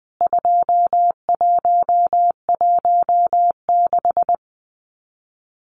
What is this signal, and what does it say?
Morse code, heard as a keyed tone, about 20 wpm, 700 Hz. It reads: 2116
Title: CWT Exchange 20wpm